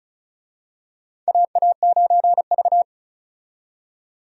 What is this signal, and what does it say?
Morse code, heard as a keyed tone, about 36 wpm, 700 Hz. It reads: AA9V
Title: USA Call Signs - 4 Characters - 35wpm